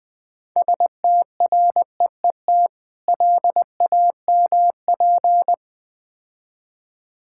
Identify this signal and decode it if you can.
Morse code, heard as a keyed tone, about 22 wpm, 700 Hz. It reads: STREET LAMP
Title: Easy Words - Speed-Racing with Hints - 20wpm